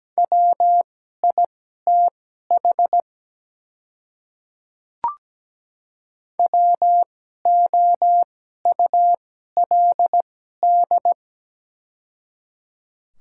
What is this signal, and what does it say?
Morse code, heard as a keyed tone, about 17 wpm, spaced as for 12 wpm, 700 Hz. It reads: WITH WOULD
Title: 2x Character Spacing - Top 100 Words - 17wpm